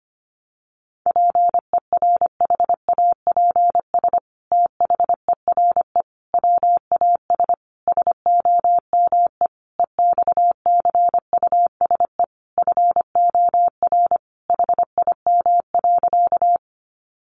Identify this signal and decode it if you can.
Morse code, heard as a keyed tone, about 25 wpm, 700 Hz. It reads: PERHAPS THERE WAS SOME EXCUSE FOR HIM.
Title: Sentences from Top 700 Words 25wpm